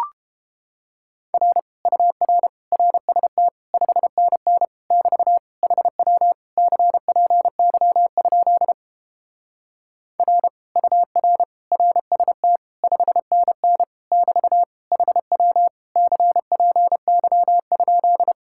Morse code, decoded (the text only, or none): R UR RST 5NN <BT> HW CPY? R UR RST 5NN <BT> HW CPY?